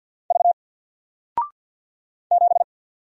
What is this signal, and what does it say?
Morse code, heard as a keyed tone, about 48 wpm, 700 Hz. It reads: V 7
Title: Single Letter-Number Alt 50wpm